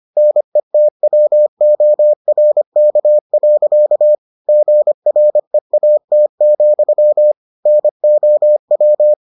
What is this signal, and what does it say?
Morse code, heard as a keyed tone, about 25 wpm, 600 Hz. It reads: NETWORK. GREAT, NOW